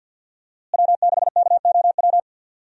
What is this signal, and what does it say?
Morse code, heard as a keyed tone, about 51 wpm, 700 Hz. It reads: W6KOW
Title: USA Call Signs - 5 Characters - Repeated 2x - 50wpm